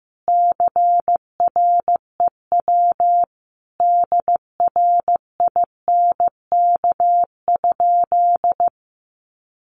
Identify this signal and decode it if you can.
Morse code, heard as a keyed tone, about 15 wpm, 700 Hz. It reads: CREW DRINK?
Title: Sentences from Top 1500 Words 15wpm